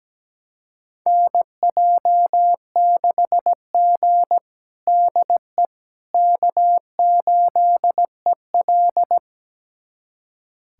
Morse code, heard as a keyed tone, about 17 wpm, 700 Hz. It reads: NJ6G DE K8EL
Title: USA Callsign DE Callsign - 4 Characters - 17wpm